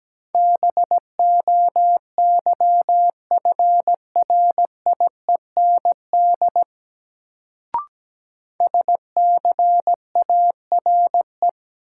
Morse code, heard as a keyed tone, about 17 wpm, 700 Hz. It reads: BOYFRIEND SCARE
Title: Sentences from Top 1000 Words - Review Part 5 - New Word Drill - 17wpm